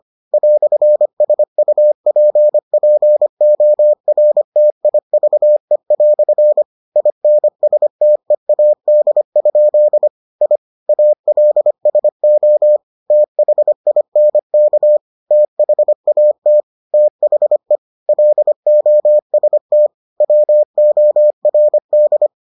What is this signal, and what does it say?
Morse code, heard as a keyed tone, about 25 wpm, 600 Hz. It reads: "SUPPORTIVE" INSTEAD? I ALSO THINK THAT THE LOST WORD